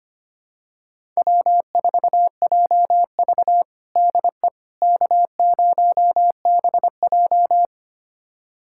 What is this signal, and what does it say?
Morse code, heard as a keyed tone, about 25 wpm, 700 Hz. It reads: W4JV DE K0BJ
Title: USA Callsign DE Callsign - 4 Characters - 25wpm